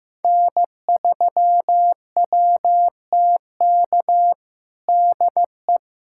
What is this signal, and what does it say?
Morse code, heard as a keyed tone, about 15 wpm, 700 Hz. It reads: N3WTK DE
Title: USA Callsign DE Callsign 15wpm